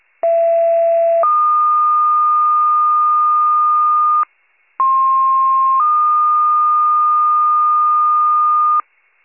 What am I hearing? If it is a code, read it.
Two-tone page, heard as a one-second tone, then a three-second tone, about 670 then 1220 Hz, about 1060 then 1220 Hz. It is heard from 0.2 to 8.8 s.